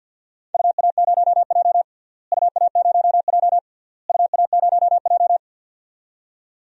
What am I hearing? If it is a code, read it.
Morse code, heard as a keyed tone, about 46 wpm, 700 Hz. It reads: UA0J UA0J UA0J
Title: DXCC-100 Call Signs - 4 Characters - Repeated 3x - 50wpm